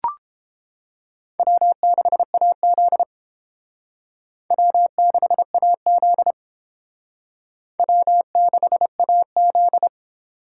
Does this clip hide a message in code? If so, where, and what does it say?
Yes: Morse code from 1.4 to 9.9 s, heard as a keyed tone, about 29 wpm, 700 Hz. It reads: W6AZ W6AZ W6AZ